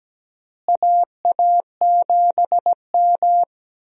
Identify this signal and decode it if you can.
Morse code, heard as a keyed tone, about 17 wpm, 700 Hz. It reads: AA7M
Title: USA Call Signs - 4 Characters - Repeated 3x - 17wpm